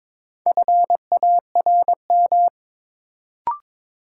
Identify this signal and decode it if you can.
Morse code, heard as a keyed tone, about 21 wpm, 700 Hz. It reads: FARM
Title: Sentences from Top 1000 Words - Review Part 2 - New Word Drill - 22wpm